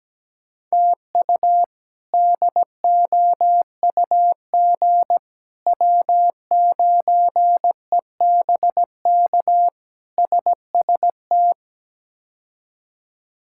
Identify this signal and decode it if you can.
Morse code, heard as a keyed tone, about 17 wpm, 700 Hz. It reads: TU DOUG W9EBK SST
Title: K1USN SST QSO Practice - 17wpm